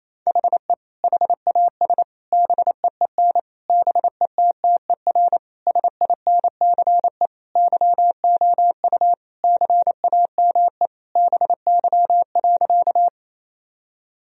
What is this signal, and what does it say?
Morse code, heard as a keyed tone, about 28 wpm, 700 Hz. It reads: HE HAS BEEN BETTER SINCE YOU CAME BY.
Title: Sentences from Top 1000 Words - Review Part 1 - 28wpm